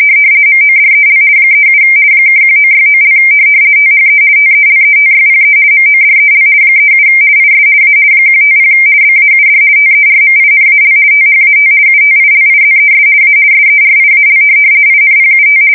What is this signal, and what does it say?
RTTY, about 50 baud. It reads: MODEL 37 (1969) THE TELETYPE MODEL 37 IS A 150 BAUD TERMINAL THAT UTILIZES A SERIAL INPUT
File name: fsk170-50B / teletype-corp